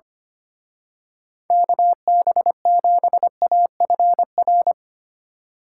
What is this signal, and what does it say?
Morse code, heard as a keyed tone, about 28 wpm, 700 Hz. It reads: KB7AFR
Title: Speed Racing - USA 2x3 Call Signs 25wpm